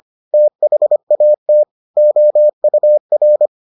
Morse code, heard as a keyed tone, about 25 wpm, 600 Hz. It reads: THAT OUR